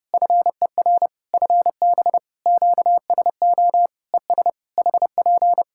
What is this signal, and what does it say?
Morse code, heard as a keyed tone, about 30 wpm, 700 Hz. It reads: FER FB QSO ES HP
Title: Practice QSO - Exchange 4 - Ending - 30wpm